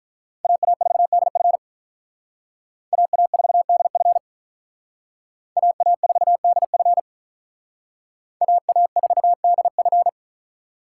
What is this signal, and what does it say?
Morse code, heard as a keyed tone, about 45 wpm, 700 Hz. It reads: AA4DF AA4DF AA4DF AA4DF